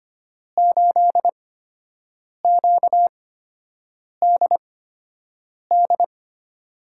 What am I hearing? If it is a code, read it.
Morse code, heard as a keyed tone, about 25 wpm, spaced as for 7 wpm, 700 Hz. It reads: 8QDD
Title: USA Call Signs - 5 Characters - 8x Character Spacing - 25wpm